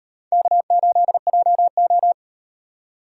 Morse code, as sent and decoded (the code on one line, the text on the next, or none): -.- ---.. .--- ---
K8JO